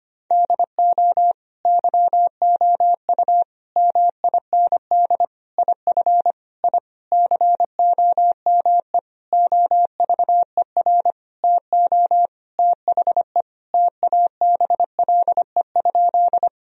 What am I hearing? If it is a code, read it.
Morse code, heard as a keyed tone, about 25 wpm, 700 Hz. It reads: DO YOU MIND IF I COME OVER TO THE TABLE?
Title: Sentences from Top 1000 Words - Review Part 2 - 25wpm